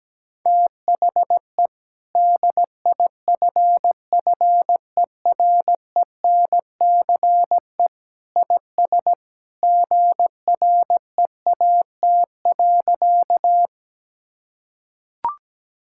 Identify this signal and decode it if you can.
Morse code, heard as a keyed tone, about 17 wpm, 700 Hz. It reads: THE DIFFERENCE IS GREAT.
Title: Sentences from Top 1000 Words - Review Part 5 - 17wpm